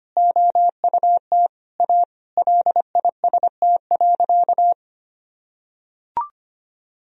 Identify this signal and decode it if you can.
Morse code, heard as a keyed tone, about 25 wpm, 700 Hz. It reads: OUT A LIST.
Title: Sentences from Top 1000 Words - Review Part 2 - 25wpm